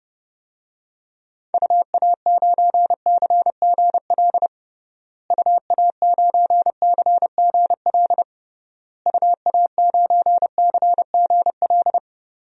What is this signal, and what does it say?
Morse code, heard as a keyed tone, about 30 wpm, 700 Hz. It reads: UA9CGL UA9CGL UA9CGL
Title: DXCC-100 Call Signs - 6 Characters - Repeated 3x - 30wpm